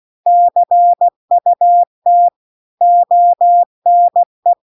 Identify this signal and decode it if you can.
Morse code, heard as a keyed tone, about 16 wpm, 700 Hz. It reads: CUT ONE
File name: CW-The-Big-Twoheaded-River-2-16WPM